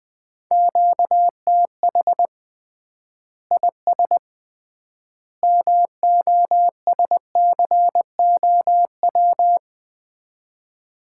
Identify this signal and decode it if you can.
Morse code, heard as a keyed tone, about 20 wpm, 700 Hz. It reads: QTH IS MOSCOW
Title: Common QSO Phrases - 3x Word-Spacing - 20wpm